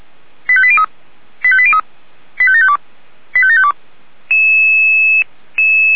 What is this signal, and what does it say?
Five-tone selective call (ZVEI), heard as quick runs of five tones, tones of about 70 ms. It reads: 85803 85803 85752 85752
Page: SDR/ZVEI